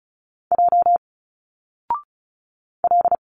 Morse code, heard as a keyed tone, about 33 wpm, 700 Hz. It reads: J L
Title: Single Letters 35wpm